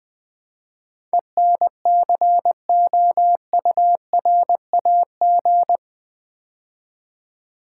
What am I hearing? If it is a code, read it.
Morse code, heard as a keyed tone, about 20 wpm, 700 Hz. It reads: ENCOURAG
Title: Build A Word - 4 Letters At A Time - 11-Letter Words - 20wpm